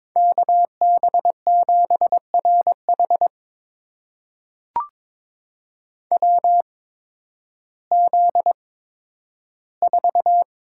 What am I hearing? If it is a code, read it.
Morse code, heard as a keyed tone, about 21 wpm, spaced as for 12 wpm, 700 Hz. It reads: KB7RH WZ4
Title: USA Call Signs - 5 Characters - 8x Character Spacing - 22wpm